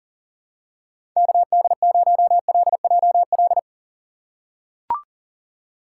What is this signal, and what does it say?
Morse code, heard as a keyed tone, about 40 wpm, 700 Hz. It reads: KD0LJL
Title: USA Call Signs - 6 Characters - 6x Character Spacing - 40wpm